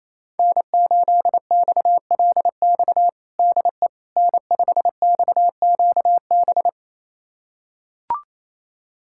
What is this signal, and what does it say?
Morse code, heard as a keyed tone, about 28 wpm, 700 Hz. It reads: N8XLX DE N5XQB
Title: USA Callsign DE Callsign - 5 Characters - 28wpm